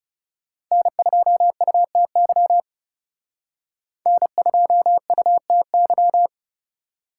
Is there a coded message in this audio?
Yes: Morse code from 0.7 to 6.3 s, heard as a keyed tone, about 38 wpm, 700 Hz. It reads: N2UTY N2UTY